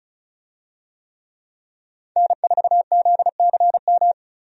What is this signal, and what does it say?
Morse code, heard as a keyed tone, about 36 wpm, 700 Hz. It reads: N4ZCM
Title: USA Call Signs - 5 Characters - 35wpm